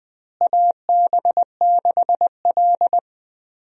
Morse code, heard as a keyed tone, about 20 wpm, 690 Hz. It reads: AB6L
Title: USA Call Signs - 4 Characters - Repeated 3x - 20wpm